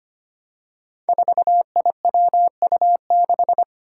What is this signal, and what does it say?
Morse code, heard as a keyed tone, about 25 wpm, 700 Hz. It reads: 4IWU6
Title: Five Challenging Letter-Number Combinations 25wpm